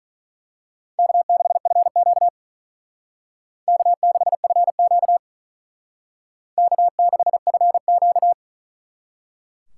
Morse code, heard as a keyed tone, about 40 wpm, 700 Hz. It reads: K6FQ K6FQ K6FQ
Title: Speed Racing - USA 1x2 Call Signs 35wpm